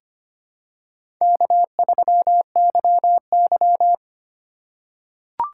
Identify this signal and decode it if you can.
Morse code, heard as a keyed tone, about 25 wpm, 700 Hz. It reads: K3YY
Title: USA Call Signs - 4 Characters - 8x Character Spacing - 25wpm